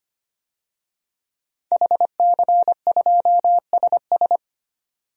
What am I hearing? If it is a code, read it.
Morse code, heard as a keyed tone, about 25 wpm, 690 Hz. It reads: HC2SS
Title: DXCC-100 Call Signs - 5 Characters - 25wpm